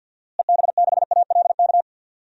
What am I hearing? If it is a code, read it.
Morse code, heard as a keyed tone, about 48 wpm, 700 Hz. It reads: EB6ALK
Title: DXCC-100 Call Signs - 6 Characters - Repeated 3x - 50wpm